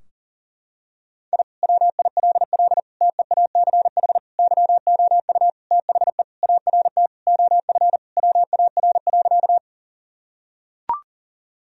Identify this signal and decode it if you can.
Morse code, heard as a keyed tone, about 37 wpm, 700 Hz. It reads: I WILL TEACH YOU THE ART OF WAR.
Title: Sentences from Top 1000 Words - Review Part 3 - 40wpm